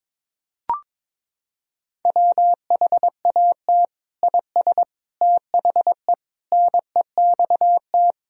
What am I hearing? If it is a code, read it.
Morse code, heard as a keyed tone, about 21 wpm, 700 Hz. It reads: WHAT IS THE NEXT
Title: Sentences from Top 1000 Words - Review Part 1 - 22wpm